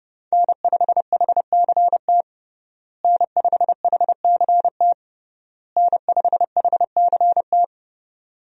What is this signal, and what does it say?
Morse code, heard as a keyed tone, about 32 wpm, 700 Hz. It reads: N5HCT N5HCT N5HCT
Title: USA Call Signs - 5 Characters - Repeated 3x - 30wpm